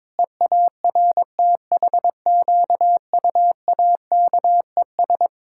Morse code, heard as a keyed tone, about 22 wpm, 700 Hz. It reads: EARTHQUAKES
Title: Pangram Challenge v2 - 22wpm